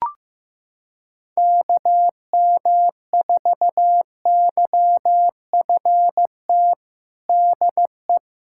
Morse code, heard as a keyed tone, about 15 wpm, 700 Hz. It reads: KM4YFT DE
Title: USA Callsign DE Callsign - 6 Characters - 15wpm